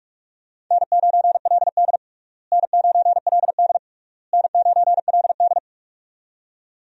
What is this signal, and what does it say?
Morse code, heard as a keyed tone, about 46 wpm, 700 Hz. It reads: N9LD N9LD N9LD
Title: USA Call Signs - 4 Characters - Repeated 3x - 45wpm